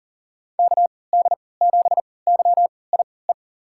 Morse code, heard as a keyed tone, about 40 wpm, spaced as for 23 wpm, 700 Hz. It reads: KD7YIE
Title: USA Call Signs - 3x Character Spacing - 40wpm